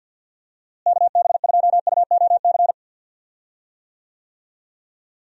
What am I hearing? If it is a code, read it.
Morse code, heard as a keyed tone, about 50 wpm, 700 Hz. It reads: KB2UOC
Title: USA Call Signs - 6 Characters - Repeated 2x - 50wpm